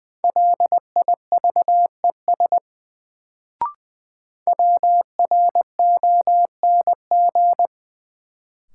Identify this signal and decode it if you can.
Morse code, heard as a keyed tone, about 20 wpm, 700 Hz. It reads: LIVES WRONG
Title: Sets of 5 Words Encore - New Word Drill - 20wpm